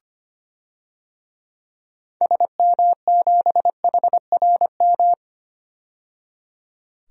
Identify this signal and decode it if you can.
Morse code, heard as a keyed tone, about 25 wpm, 700 Hz. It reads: SM7HRM
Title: DXCC-100 Call Signs - 6 Characters - 25wpm